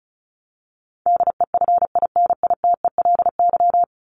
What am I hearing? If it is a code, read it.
Morse code, heard as a keyed tone, about 34 wpm, 700 Hz. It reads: DEFINITELY
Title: Top 1751-2000 Words 35wpm